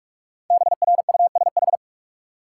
Morse code, heard as a keyed tone, about 46 wpm, 700 Hz. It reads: BRUSH
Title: Sentences from Top 1100 Words - New Word Drill Alt - 45wpm